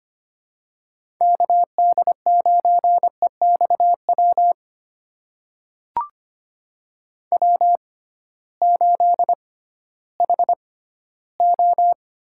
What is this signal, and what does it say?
Morse code, heard as a keyed tone, about 25 wpm, spaced as for 15 wpm, 700 Hz. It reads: KD9EXW W8HO
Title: USA Call Signs - 6x Character Spacing - 25wpm